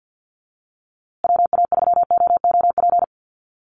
Morse code, heard as a keyed tone, about 50 wpm, 700 Hz. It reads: WA3OOP
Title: USA Call Signs Repeated 3x 50wpm